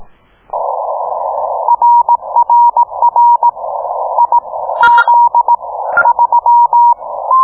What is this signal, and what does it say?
Morse code, heard as a keyed tone, about 18 wpm, spaced as for 14 wpm, 950 Hz. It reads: RRR IZ3